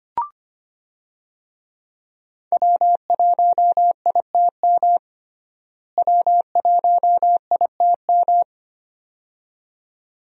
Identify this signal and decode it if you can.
Morse code, heard as a keyed tone, about 25 wpm, 700 Hz. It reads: W1ITM W1ITM
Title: USA Call Signs - 5 Characters - Repeated 2x - 25wpm